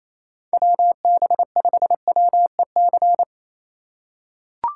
Morse code, heard as a keyed tone, about 28 wpm, 700 Hz. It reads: WB5WEC
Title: K1USN SST QSO Practice - 28wpm